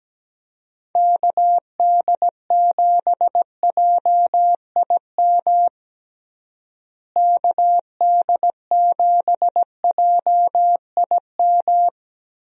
Morse code, heard as a keyed tone, about 17 wpm, 700 Hz. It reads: KD7JIM KD7JIM
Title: USA Call Signs - 6 Characters - Repeated 3x - 17wpm